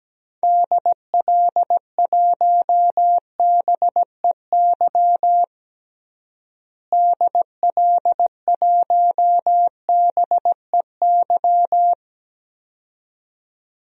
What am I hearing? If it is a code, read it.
Morse code, heard as a keyed tone, about 17 wpm, 700 Hz. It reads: DL1BEY DL1BEY
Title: DXCC-100 Call Signs - 6 Characters - Repeated 2x - 17wpm